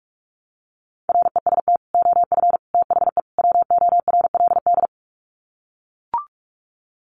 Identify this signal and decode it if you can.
Morse code, heard as a keyed tone, about 44 wpm, 700 Hz. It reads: REST OF THE WORLD
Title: Sets of 4 Words 45wpm